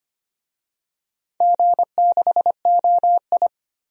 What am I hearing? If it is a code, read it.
Morse code, heard as a keyed tone, about 25 wpm, 700 Hz. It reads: G6OI
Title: DXCC-100 Call Signs - 4 Characters - 25wpm